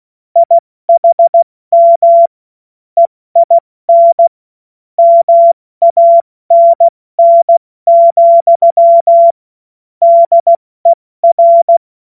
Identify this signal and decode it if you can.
Morse code, heard as a keyed tone, about 16 wpm, 680 Hz. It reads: IHM EIN MANN, DER
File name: Morsekurs / Witze / Witz 16 Januar 2024 70 F80